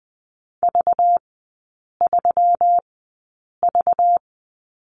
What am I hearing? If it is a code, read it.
Morse code, heard as a keyed tone, about 20 wpm, spaced as for 8 wpm, 700 Hz. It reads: V3V
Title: 191 Lesson 31e - Letter v - Three Characters 20wpm